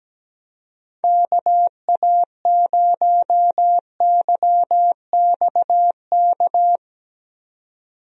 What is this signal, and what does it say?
Morse code, heard as a keyed tone, about 17 wpm, 700 Hz. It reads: KA0YXK
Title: USA Call Signs - 6 Characters - 17wpm